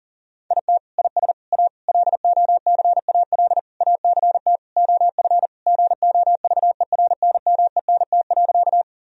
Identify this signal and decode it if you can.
Morse code, heard as a keyed tone, about 40 wpm, 700 Hz. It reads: IT IS A LOCAL ACT OF GOVERNMENT.
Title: Sentences from Top 1000 Words - Review Part 3 - 40wpm